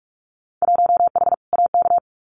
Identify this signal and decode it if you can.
Morse code, heard as a keyed tone, about 46 wpm, 700 Hz. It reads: 1H AK
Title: Winter Field Day Exchanges Encore 45wpm